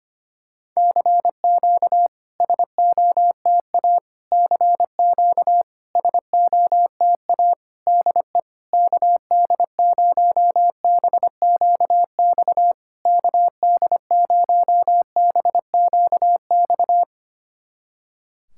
Morse code, heard as a keyed tone, about 25 wpm, 700 Hz. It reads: CQ SOTA CQ SOTA DE KD0BQX KD0BQX
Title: SOTA QSO Practice 25wpm